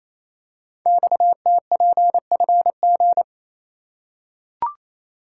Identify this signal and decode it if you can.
Morse code, heard as a keyed tone, about 27 wpm, 700 Hz. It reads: XTPFG
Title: Code Groups - 5 Letters - 28wpm